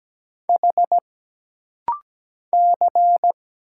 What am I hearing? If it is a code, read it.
Morse code, heard as a keyed tone, about 17 wpm, 700 Hz. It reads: H C
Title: Single Letter-Number - Rapid-Fire 17wpm